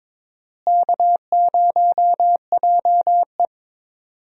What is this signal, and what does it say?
Morse code, heard as a keyed tone, about 22 wpm, 700 Hz. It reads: K0JE
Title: USA Call Signs - 4 Characters - Repeated 2x - 22wpm